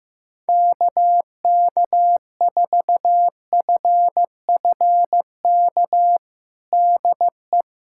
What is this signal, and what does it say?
Morse code, heard as a keyed tone, about 15 wpm, 700 Hz. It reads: KK4FFK DE
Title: USA Callsign DE Callsign - 6 Characters - 15wpm